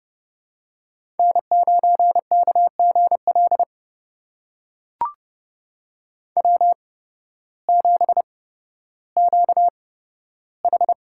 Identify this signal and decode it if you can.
Morse code, heard as a keyed tone, about 30 wpm, spaced as for 14 wpm, 700 Hz. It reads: N9KGL W7QH
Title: USA Call Signs - 8x Character Spacing - 30wpm